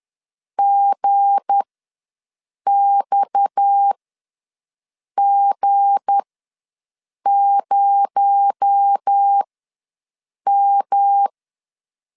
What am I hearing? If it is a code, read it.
Morse code, heard as a keyed tone, about 10 wpm, spaced as for 6 wpm, 800 Hz. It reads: GXG0M
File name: LEZIONI / 7 GF9Z / 7 G ws 10wpm 3843